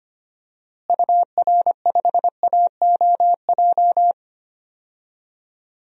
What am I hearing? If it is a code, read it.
Morse code, heard as a keyed tone, about 25 wpm, 700 Hz. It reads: UR5AOJ
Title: DXCC-100 Call Signs - 6 Characters - Repeated 2x - 25wpm